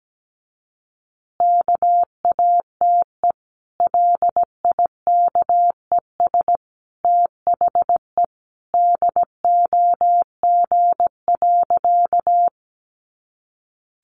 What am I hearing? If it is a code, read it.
Morse code, heard as a keyed tone, about 17 wpm, 700 Hz. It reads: KATE LIKES THE DOG.
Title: Easy Phrases 17wpm